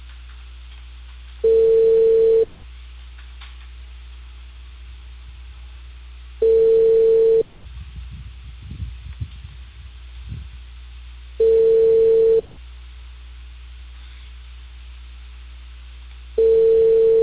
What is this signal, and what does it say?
Ringback tone (China, 450 Hz).